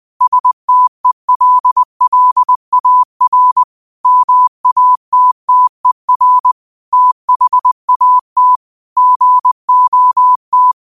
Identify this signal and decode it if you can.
Morse code, heard as a keyed tone, about 20 wpm, 1000 Hz. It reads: STELLAR MATTER THAT GOT
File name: qotdfiles / 20wpm